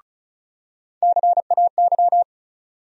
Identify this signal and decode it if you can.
Morse code, heard as a keyed tone, about 35 wpm, 700 Hz. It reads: CAY
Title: USA Call Sign Postfixes - 3 Letters 35wpm